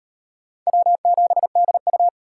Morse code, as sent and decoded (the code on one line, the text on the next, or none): .-- --... -.. ..-
W7DU